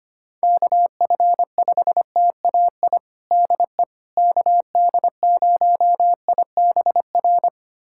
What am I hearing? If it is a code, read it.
Morse code, heard as a keyed tone, about 25 wpm, 700 Hz. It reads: KF5TAI DE KD0IBR
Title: USA Callsign DE Callsign 25wpm